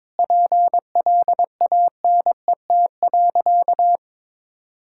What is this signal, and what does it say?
Morse code, heard as a keyed tone, about 22 wpm, 700 Hz. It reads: PLANET.
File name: Difficult Sentences with Hints - 22wpm